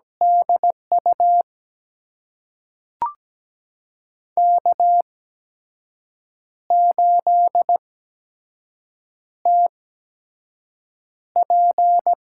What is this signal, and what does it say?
Morse code, heard as a keyed tone, about 17 wpm, spaced as for 6 wpm, 700 Hz. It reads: DU K8TP
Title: USA Call Signs - 8x Character Spacing - 17wpm